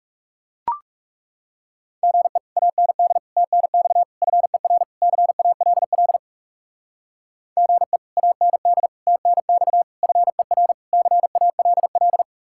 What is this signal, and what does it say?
Morse code, heard as a keyed tone, about 38 wpm, 700 Hz. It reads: GE AND TNX FER CALL GE AND TNX FER CALL